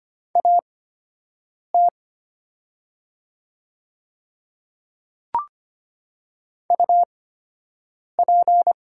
Morse code, heard as a keyed tone, about 25 wpm, spaced as for 7 wpm, 690 Hz. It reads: AT UP
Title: 8x Character Spacing - Top 100 Words - 25wpm